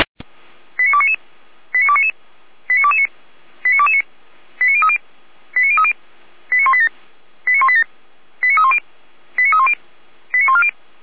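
Five-tone selective call (ZVEI), heard as quick runs of five tones, tones of about 70 ms. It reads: 89299 89299 89229 89229 89030 89030 89187 89187 89310 89310 89140
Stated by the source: ZVEI five-tone call received on an SDR.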